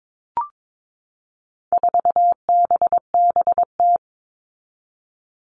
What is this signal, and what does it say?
Morse code, heard as a keyed tone, about 22 wpm, 700 Hz. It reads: 4BBT
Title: All Cut Numbers - 4-Digit 22wpm